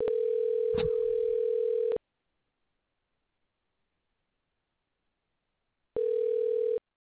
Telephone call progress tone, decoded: ringback tone

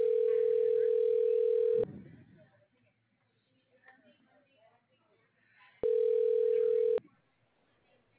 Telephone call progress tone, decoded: ringback tone

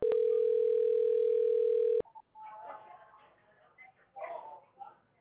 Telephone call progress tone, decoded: ringback tone

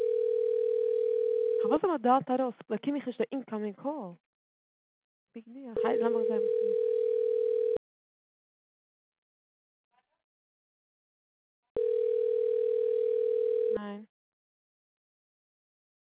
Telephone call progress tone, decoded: ringback tone

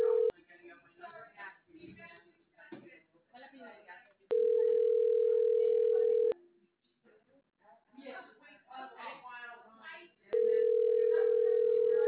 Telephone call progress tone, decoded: ringback tone

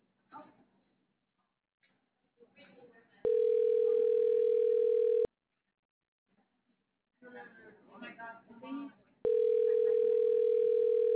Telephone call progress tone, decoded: ringback tone

